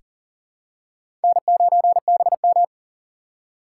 Morse code, heard as a keyed tone, about 40 wpm, 700 Hz. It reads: N9BM
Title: USA Call Signs - 4 Characters - 4x Character Spacing - 40wpm